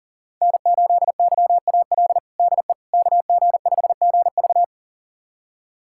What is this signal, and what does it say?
Morse code, heard as a keyed tone, about 40 wpm, 700 Hz. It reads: N8YAL DE KG5GV
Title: USA Callsign DE Callsign - 5 Characters - 40wpm